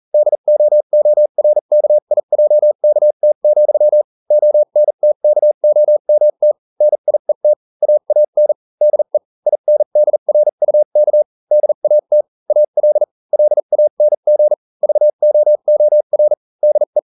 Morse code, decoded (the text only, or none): DOORKIJKT, ONTKOMT NIET AAN DE INDRUK DAT AL LANG VOOR DE